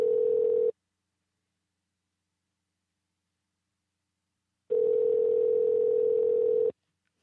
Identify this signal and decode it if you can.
Ringback tone (North America), 440 + 480 Hz.